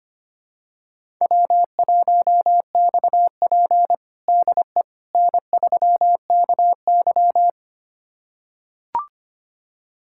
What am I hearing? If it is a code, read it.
Morse code, heard as a keyed tone, about 25 wpm, 700 Hz. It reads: W1XP DE N3KY
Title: USA Callsign DE Callsign - 4 Characters - 25wpm